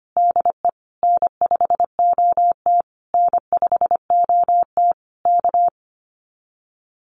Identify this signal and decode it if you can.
Morse code, heard as a keyed tone, about 25 wpm, 700 Hz. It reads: DE N5OT N5OT K
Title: Common QSO Phrases Encore 25wpm